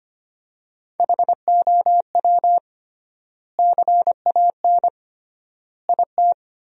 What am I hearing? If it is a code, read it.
Morse code, heard as a keyed tone, about 25 wpm, 700 Hz. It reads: HOW CAN IT
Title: 3x Word-Spacing - Sentences from Top 100 Words - 25wpm